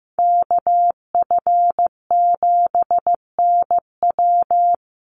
Morse code, heard as a keyed tone, about 15 wpm, 700 Hz. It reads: KF7NW
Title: USA Call Signs 15wpm